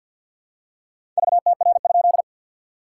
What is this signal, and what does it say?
Morse code, heard as a keyed tone, about 50 wpm, 700 Hz. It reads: VTR?
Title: SOTA QSO Practice 50wpm